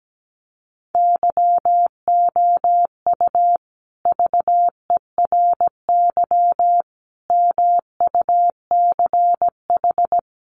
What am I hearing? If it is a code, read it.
Morse code, heard as a keyed tone, about 17 wpm, 700 Hz. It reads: YOU VERY MUCH
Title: Sets of 5 Words Encore 17wpm